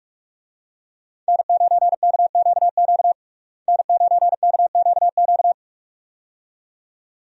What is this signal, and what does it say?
Morse code, heard as a keyed tone, about 46 wpm, 700 Hz. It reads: N9KQQ N9KQQ
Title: USA Call Signs - 5 Characters - Repeated 2x - 45wpm